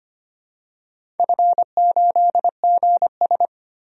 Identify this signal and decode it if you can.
Morse code, heard as a keyed tone, about 25 wpm, 700 Hz. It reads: F8GS
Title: DXCC-100 Call Signs - 4 Characters - 25wpm